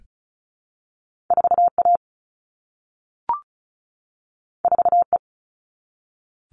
Morse code, heard as a keyed tone, about 36 wpm, 700 Hz. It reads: 4A 4E
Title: All Cut Numbers - 2-Digit 35wpm